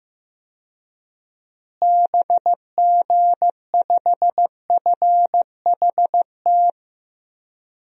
Morse code, heard as a keyed tone, about 15 wpm, 700 Hz. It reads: BG5FHT
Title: DXCC-100 Call Signs - 6 Characters - Repeated 3x - 15wpm